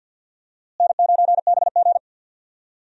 Morse code, heard as a keyed tone, about 52 wpm, 700 Hz. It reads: N9BG
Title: USA Call Signs - 4 Characters - Repeated 3x - 50wpm